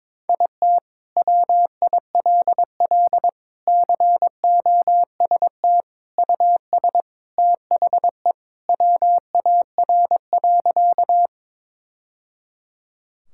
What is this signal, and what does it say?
Morse code, heard as a keyed tone, about 22 wpm, 700 Hz. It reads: IT WILL COST US THE WAR.
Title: Sentences from Top 1000 Words - Review Part 5 - 22wpm